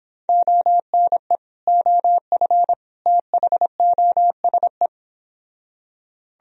Morse code, heard as a keyed tone, about 26 wpm, 700 Hz. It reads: ONE OF THOSE